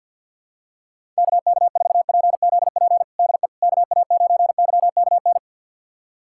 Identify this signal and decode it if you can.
Morse code, heard as a keyed tone, about 46 wpm, 700 Hz. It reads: KK4PZP DE KA9YKN
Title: USA Callsign DE Callsign - 6 Characters - 50wpm